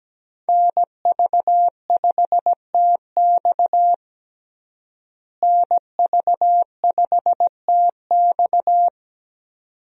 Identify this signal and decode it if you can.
Morse code, heard as a keyed tone, about 17 wpm, 700 Hz. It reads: NV5TX NV5TX
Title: USA Call Signs - 5 Characters - Repeated 2x - 17wpm